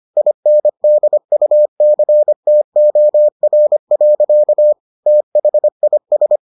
Morse code, heard as a keyed tone, about 25 wpm, 600 Hz. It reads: INDUCTOR. THIS